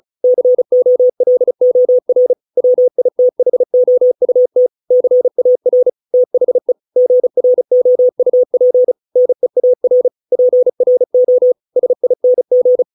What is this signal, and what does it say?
Morse code, heard as a keyed tone, about 34 wpm, 500 Hz. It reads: COLOR WITHOUT CAR THE GROUP NEAR PRO SING